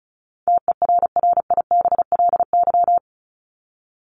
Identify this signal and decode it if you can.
Morse code, heard as a keyed tone, about 34 wpm, 700 Hz. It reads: TERRIBLY
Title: Top 1751-2000 Words 35wpm